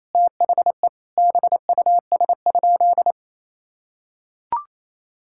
Morse code, heard as a keyed tone, about 28 wpm, 700 Hz. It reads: THE BUS?
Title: Sentences from Top 1000 Words - Review Part 6 - 28wpm